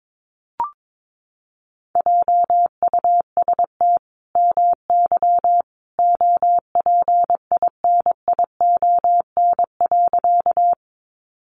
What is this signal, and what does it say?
Morse code, heard as a keyed tone, about 22 wpm, 700 Hz. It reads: JUST MY OPINION.
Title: Sentences from Top 1200 Words 22wpm